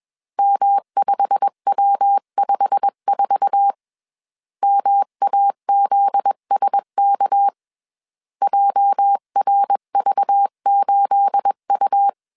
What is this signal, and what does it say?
Morse code, heard as a keyed tone, about 21 wpm, 800 Hz. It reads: M5W54 MAZSK JRV8U
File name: CW / DET / 1 9 1500C S B 20wpm 2313